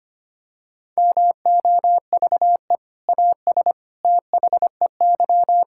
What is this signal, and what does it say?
Morse code, heard as a keyed tone, about 25 wpm, 700 Hz. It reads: MOVE AS THEY
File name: practice / 25wpm / Sentences with 2-Word Phrases - 25wpm